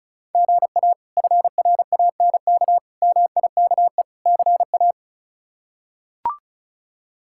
Morse code, heard as a keyed tone, about 36 wpm, 700 Hz. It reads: GA FRANK MIKE CA
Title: K1USN SST QSO Practice - 35wpm